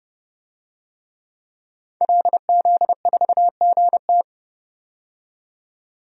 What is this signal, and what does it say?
Morse code, heard as a keyed tone, about 30 wpm, 700 Hz. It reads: LZ4GT
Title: DXCC-100 Call Signs - 5 Characters - 30wpm